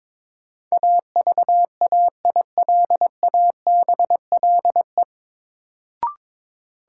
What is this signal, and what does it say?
Morse code, heard as a keyed tone, about 22 wpm, 700 Hz. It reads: AVAILABLE
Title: Sentences from Top 600 Words - New Word Drill - 22wpm